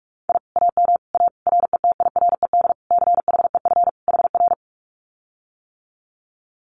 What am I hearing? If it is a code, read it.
Morse code, heard as a keyed tone, about 44 wpm, 700 Hz. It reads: I AM A RETIRED CHEF HR
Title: Common QSO Phrases Encore 45wpm